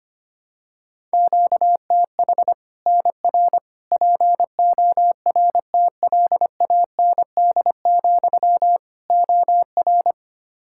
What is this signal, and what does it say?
Morse code, heard as a keyed tone, about 25 wpm, 700 Hz. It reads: QTH NR PORTLAND, OR
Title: Practice QSO - Exchange 1 - Beginning - 25wpm